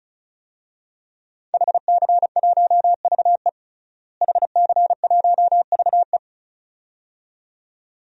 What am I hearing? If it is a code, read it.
Morse code, heard as a keyed tone, about 36 wpm, 700 Hz. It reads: HC1VE HC1VE
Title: DXCC-100 Call Signs - 5 Characters - Repeated 2x - 35wpm